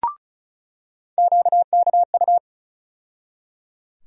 Morse code, heard as a keyed tone, about 34 wpm, 700 Hz. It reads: QKU
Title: USA Call Sign Postfixes - 3 Letters 35wpm